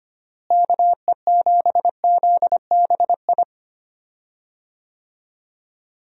Morse code, heard as a keyed tone, about 25 wpm, 700 Hz. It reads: KE7ZBI
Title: USA Call Signs - 6 Characters - Repeated 2x - 25wpm